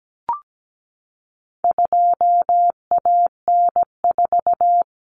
Morse code, heard as a keyed tone, about 17 wpm, 700 Hz. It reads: 2AN4
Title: Common Cut Numbers - 4-Digit 17wpm